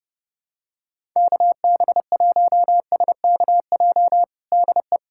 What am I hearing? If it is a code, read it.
Morse code, heard as a keyed tone, about 30 wpm, 700 Hz. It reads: KB1SKJ DE
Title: USA Callsign DE Callsign 30wpm